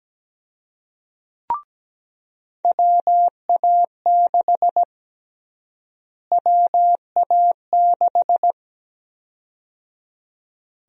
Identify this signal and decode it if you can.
Morse code, heard as a keyed tone, about 17 wpm, 700 Hz. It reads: WA6 WA6
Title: USA Call Sign Prefixes - Repeated 2x - 17wpm